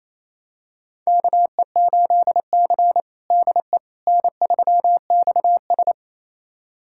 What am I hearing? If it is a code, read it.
Morse code, heard as a keyed tone, about 28 wpm, 700 Hz. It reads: KE8C DE N3XS
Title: USA Callsign DE Callsign - 4 Characters - 28wpm